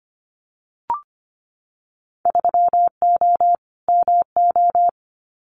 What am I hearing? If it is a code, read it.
Morse code, heard as a keyed tone, about 25 wpm, 700 Hz. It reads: 3O MO
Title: Winter Field Day Exchanges Encore 25wpm